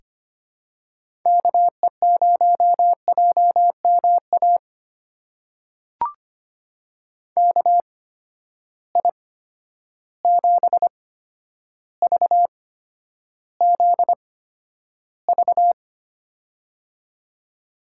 Morse code, heard as a keyed tone, about 25 wpm, spaced as for 11 wpm, 700 Hz. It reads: KE0JMA KI7VZV
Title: USA Call Signs - 6 Characters - 8x Character Spacing - 25wpm